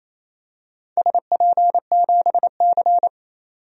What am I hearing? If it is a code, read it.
Morse code, heard as a keyed tone, about 28 wpm, 700 Hz. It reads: SP7C
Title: DXCC-100 Call Signs - 4 Characters - 28wpm